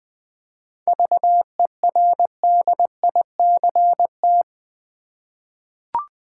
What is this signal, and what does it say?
Morse code, heard as a keyed tone, about 20 wpm, 700 Hz. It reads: VERDICT
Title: 7 Letter Words - First 3 Letters Given - 20wpm